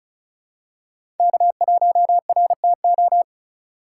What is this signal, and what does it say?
Morse code, heard as a keyed tone, about 36 wpm, 690 Hz. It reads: K1RTO
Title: USA Call Signs - 5 Characters - 3x Character Spacing - 35wpm